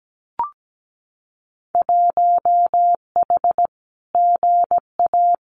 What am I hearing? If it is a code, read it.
Morse code, heard as a keyed tone, about 17 wpm, 700 Hz. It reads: 1H GA
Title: Winter Field Day Exchanges 17wpm